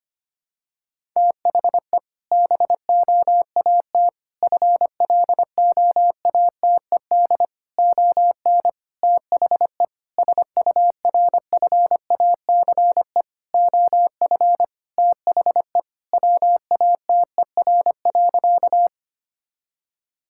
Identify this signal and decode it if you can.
Morse code, heard as a keyed tone, about 25 wpm, 700 Hz. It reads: THE BOAT FLOATED ON THE SURFACE OF THE WATER.
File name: practice / 25wpm / Sentences with 4-Word Phrases - 25wpm